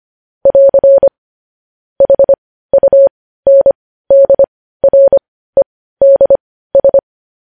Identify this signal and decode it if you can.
Morse code, heard as a keyed tone, about 25 wpm, spaced as for 15 wpm, 550 Hz. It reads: <AR> HUNDREDS
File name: podcast / News headlines-1x-25x15